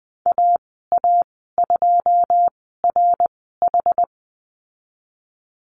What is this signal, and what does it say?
Morse code, heard as a keyed tone, about 20 wpm, spaced as for 15 wpm, 700 Hz. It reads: AA2RH
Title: 088 Lesson 15g - Number 5 - Call Signs 20wpm